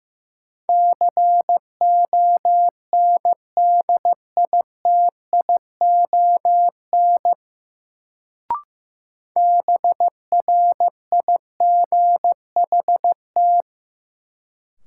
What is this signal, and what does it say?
Morse code, heard as a keyed tone, about 15 wpm, 700 Hz. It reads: CONDITION BRIGHT
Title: Sentences from Top 1000 Words - New Word Drill - 15wpm